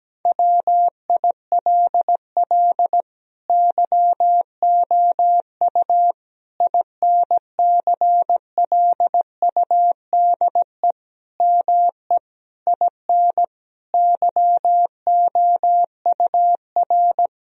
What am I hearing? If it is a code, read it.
Morse code, heard as a keyed tone, about 17 wpm, 700 Hz. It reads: WILL YOU INCLUDE ME IN YOUR
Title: Sentences from Top 1000 Words - Review Part 3 - 17wpm